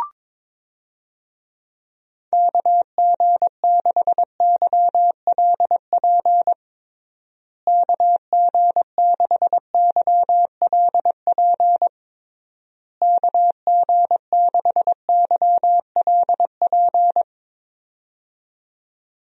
Morse code, heard as a keyed tone, about 22 wpm, 700 Hz. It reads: KG6YLP KG6YLP KG6YLP
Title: USA Call Signs - 6 Characters - Repeated 3x - 22wpm